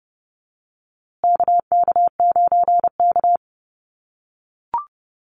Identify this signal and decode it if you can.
Morse code, heard as a keyed tone, about 30 wpm, 700 Hz. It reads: KK9K
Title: USA Call Signs Repeated 3x 30wpm